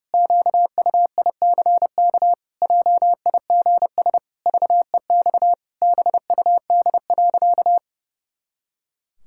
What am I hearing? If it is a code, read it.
Morse code, heard as a keyed tone, about 32 wpm, 700 Hz. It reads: QUICK JIGS VEX BUD.
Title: Pangram Challenge v2 - 30wpm